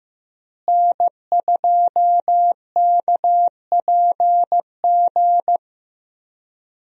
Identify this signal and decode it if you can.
Morse code, heard as a keyed tone, about 15 wpm, 700 Hz. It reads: N2KPG
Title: USA Call Signs - 5 Characters - Repeated 2x - 15wpm